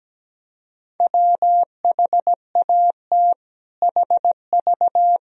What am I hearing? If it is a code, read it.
Morse code, heard as a keyed tone, about 17 wpm, 700 Hz. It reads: WHAT HV
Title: Sentences from Top 500 Words and Abbreviations 17wpm